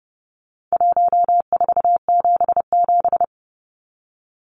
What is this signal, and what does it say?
Morse code, heard as a keyed tone, about 30 wpm, 700 Hz. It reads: 1477
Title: Numbers - Contest Style - 4-Digit 30wpm